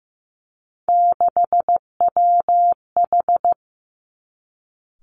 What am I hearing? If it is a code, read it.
Morse code, heard as a keyed tone, about 15 wpm, 700 Hz. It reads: 6WH
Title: Three Challenging Letter-Number Combinations 15wpm